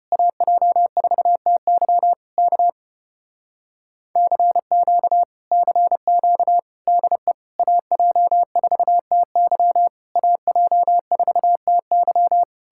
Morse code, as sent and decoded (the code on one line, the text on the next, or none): .- .--- ....- - -.-- / -.- / -.-. --.- / -.-. --.- / -.. / .- .--- ....- - -.-- / .- .--- ....- - -.--
AJ4TY K CQ CQ DE AJ4TY AJ4TY